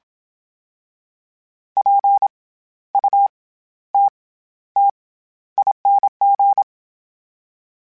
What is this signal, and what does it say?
Morse code, heard as a keyed tone, about 27 wpm, 800 Hz. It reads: P U T T ING
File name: Special INGs-mp3s / Special INGs-mp3s / SlowToFaster ing @25